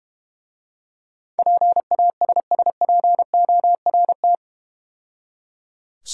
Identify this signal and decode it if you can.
Morse code, heard as a keyed tone, about 31 wpm, 700 Hz. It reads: PASSPORT
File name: Easy Words - Speed-Racing with Hints - 25wpm